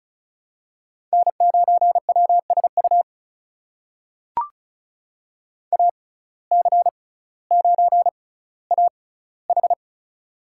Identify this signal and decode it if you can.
Morse code, heard as a keyed tone, about 36 wpm, spaced as for 18 wpm, 700 Hz. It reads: N9WSU AC9AH
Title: USA Call Signs - 6x Character Spacing - 35wpm